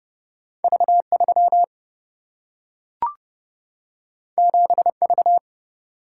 Morse code, heard as a keyed tone, about 30 wpm, 700 Hz. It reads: V3 7V
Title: Single-Letter Single-Number 30wpm